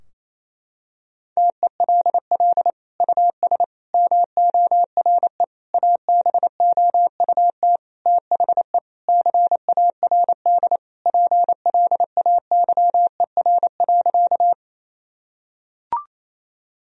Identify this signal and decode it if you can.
Morse code, heard as a keyed tone, about 28 wpm, 700 Hz. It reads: TELL US MORE ABOUT THE CARD PLAYER.
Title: Sentences from Top 1000 Words - Review Part 4 - 28wpm